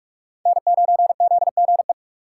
Morse code, heard as a keyed tone, about 46 wpm, 690 Hz. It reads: N9ZGE
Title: USA Call Signs - 6x Character Spacing - 45wpm